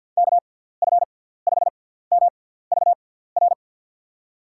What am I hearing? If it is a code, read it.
Morse code, heard as a keyed tone, about 47 wpm, spaced as for 17 wpm, 700 Hz. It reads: KF5MVR
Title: USA Call Signs - 6 Characters - 6x Character Spacing - 50wpm